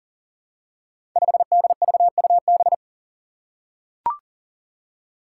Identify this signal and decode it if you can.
Morse code, heard as a keyed tone, about 40 wpm, 700 Hz. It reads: 5DVUB
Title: Five Challenging Letter-Number Combinations 40wpm